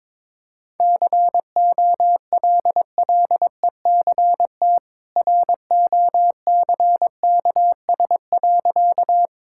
Morse code, heard as a keyed tone, about 22 wpm, 690 Hz. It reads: COLLECT ROCKS.
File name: practice / 22wpm / Easy Sentences with Hints - 22wpm